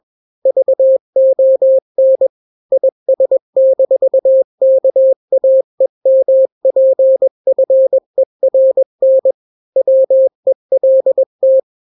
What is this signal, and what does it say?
Morse code, heard as a keyed tone, about 21 wpm, 530 Hz. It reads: VON IS-KAEMPFERN WELT
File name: daily / 20wpm